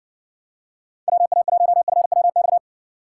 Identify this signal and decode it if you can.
Morse code, heard as a keyed tone, about 60 wpm, 700 Hz. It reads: WA1FWX